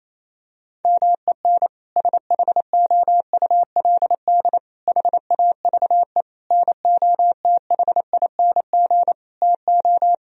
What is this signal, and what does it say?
Morse code, heard as a keyed tone, about 28 wpm, 700 Hz. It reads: MEN SHOULD HAVE NOTHING TO
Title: Sentences from Top 1000 Words - Review Part 3 - 28wpm